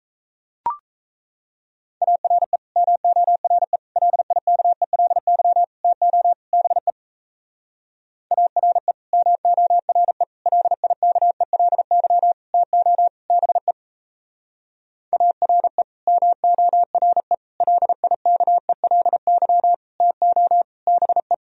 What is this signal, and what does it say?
Morse code, heard as a keyed tone, about 38 wpm, 700 Hz. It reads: ARE MORE LIKELY TO BE ARE MORE LIKELY TO BE ARE MORE LIKELY TO BE